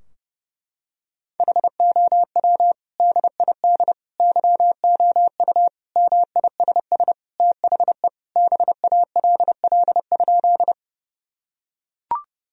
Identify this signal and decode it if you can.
Morse code, heard as a keyed tone, about 30 wpm, 700 Hz. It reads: HOW DID YOU MISS THE BALL?
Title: Sentences from Top 1000 Words - Review Part 3 - 30wpm